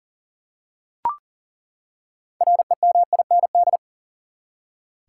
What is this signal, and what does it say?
Morse code, heard as a keyed tone, about 40 wpm, 700 Hz. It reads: REMIND
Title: Six Letter Words - 40wpm